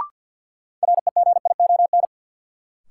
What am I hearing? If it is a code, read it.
Morse code, heard as a keyed tone, about 52 wpm, 700 Hz. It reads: REGION
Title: Sentences from Top 1500 Words - New Word Drill Alt - 50wpm